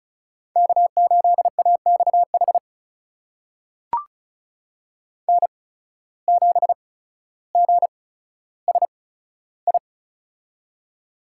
Morse code, heard as a keyed tone, about 34 wpm, spaced as for 15 wpm, 700 Hz. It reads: K8AXH N7GSI
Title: USA Call Signs - 5 Characters - 8x Character Spacing - 35wpm